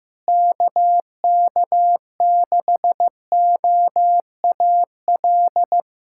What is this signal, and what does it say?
Morse code, heard as a keyed tone, about 15 wpm, 700 Hz. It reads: KK6OAL
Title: USA Callsign DE Callsign - 6 Characters - 15wpm